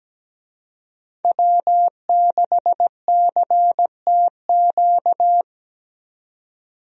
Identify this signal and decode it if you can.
Morse code, heard as a keyed tone, about 17 wpm, 700 Hz. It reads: W6CTQ
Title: USA Call Signs - 5 Characters - 17wpm